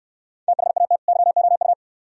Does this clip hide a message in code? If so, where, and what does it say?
Yes: Morse code from 0.5 to 1.7 s, heard as a keyed tone, about 68 wpm, 690 Hz. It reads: THAT YOU